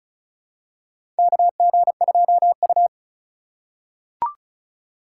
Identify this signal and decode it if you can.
Morse code, heard as a keyed tone, about 36 wpm, 700 Hz. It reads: KG2U
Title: USA Call Signs - 4 Characters - 2x Character Spacing - 35wpm